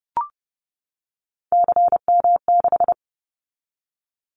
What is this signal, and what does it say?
Morse code, heard as a keyed tone, about 30 wpm, 700 Hz. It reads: CM6
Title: Callsign Prefixes for Top 100 DXCC - 30wpm